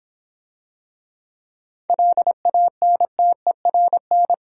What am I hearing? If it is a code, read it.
Morse code, heard as a keyed tone, about 26 wpm, 700 Hz. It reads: LANTERN